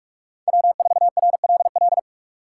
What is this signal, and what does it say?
Morse code, heard as a keyed tone, about 58 wpm, 700 Hz. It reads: W4RLL